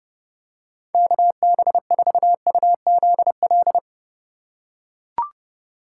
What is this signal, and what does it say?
Morse code, heard as a keyed tone, about 32 wpm, 700 Hz. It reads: KB4UZL
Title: USA Call Signs - 6x Character Spacing - 30wpm